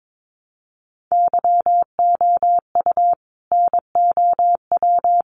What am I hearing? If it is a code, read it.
Morse code, heard as a keyed tone, about 22 wpm, 700 Hz. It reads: YOU NOW
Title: Sentences from Top 1100 Words 22wpm